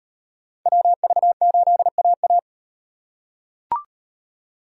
Morse code, heard as a keyed tone, about 37 wpm, 700 Hz. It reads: WV8AA